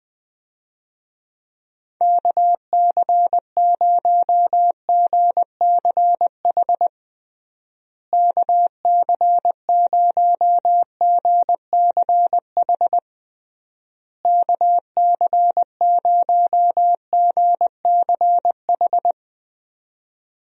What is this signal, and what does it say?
Morse code, heard as a keyed tone, about 20 wpm, 700 Hz. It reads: KC0GCH KC0GCH KC0GCH
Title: USA Call Signs - 6 Characters - Repeated 3x - 20wpm